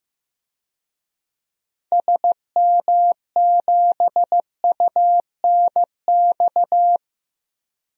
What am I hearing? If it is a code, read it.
Morse code, heard as a keyed tone, about 15 wpm, 700 Hz. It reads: SM7UNX
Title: DXCC-100 Call Signs - 6 Characters - Repeated 3x - 15wpm